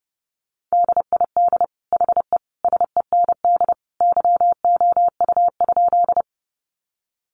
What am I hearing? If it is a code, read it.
Morse code, heard as a keyed tone, about 30 wpm, 700 Hz. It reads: DID HE SEND YOU?
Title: Sentences from Top 1500 Words 30wpm